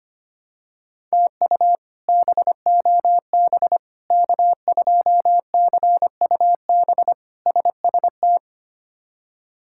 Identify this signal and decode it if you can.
Morse code, heard as a keyed tone, about 25 wpm, 700 Hz. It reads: TU BOB K2CUB SST
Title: K1USN SST QSO Practice - 25wpm